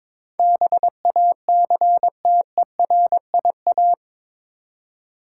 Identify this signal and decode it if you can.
Morse code, heard as a keyed tone, about 21 wpm, 700 Hz. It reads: BACTERIA
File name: practice / 22wpm / Eight Letter Words - 22wpm